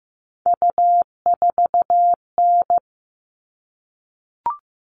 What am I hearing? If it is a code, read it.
Morse code, heard as a keyed tone, about 15 wpm, 700 Hz. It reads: U4N
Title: All Cut Numbers - 3-Digit 15wpm